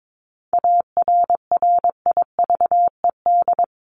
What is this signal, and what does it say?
Morse code, heard as a keyed tone, about 22 wpm, 700 Hz. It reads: ARRIVED
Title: ED Suffix - Top 100 Words with Base Words 22wpm